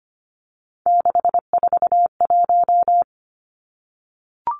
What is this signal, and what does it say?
Morse code, heard as a keyed tone, about 25 wpm, 700 Hz. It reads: 641
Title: Three Challenging Letter-Number Combinations 25wpm